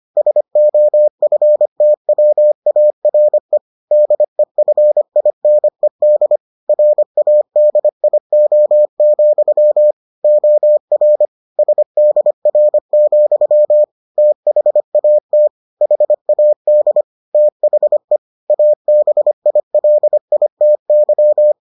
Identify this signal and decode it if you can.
Morse code, heard as a keyed tone, about 25 wpm, 600 Hz. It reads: SOFTWARE DEFINED RADIO, OR SDR, THAT HAD THE ABILITY